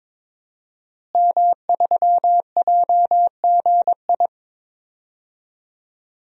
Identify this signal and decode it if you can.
Morse code, heard as a keyed tone, about 22 wpm, 700 Hz. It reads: M3JGI
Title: DXCC-100 Call Signs - 5 Characters - Repeated 2x - 22wpm